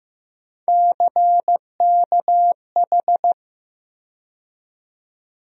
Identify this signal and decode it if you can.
Morse code, heard as a keyed tone, about 15 wpm, 700 Hz. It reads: CKH
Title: USA Call Sign Postfixes - 3 Letters 15wpm